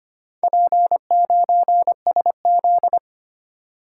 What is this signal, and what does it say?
Morse code, heard as a keyed tone, about 25 wpm, 700 Hz. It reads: P9SZ
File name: Four Letter-Number Combinations - 25wpm